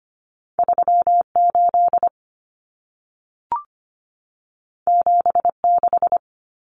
Morse code, heard as a keyed tone, about 25 wpm, 700 Hz. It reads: 38 76
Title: Common Cut Numbers - 2-Digit 25wpm